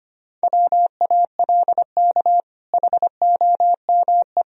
Morse code, heard as a keyed tone, about 25 wpm, 700 Hz. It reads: WALK HOME
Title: Sentences from Top 1000 Words - Review Part 1 - 25wpm